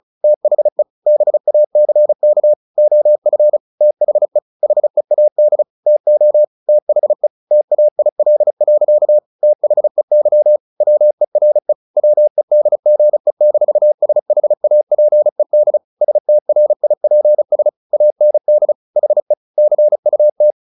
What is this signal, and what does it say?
Morse code, heard as a keyed tone, about 34 wpm, 600 Hz. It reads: THE BACK OF THE HEAD TO THE TAIL. THEY WERE WEDGE-SHAPED STRIPS AND HE CUT
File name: the old man and the sea / the old man and the sea 1 35wpm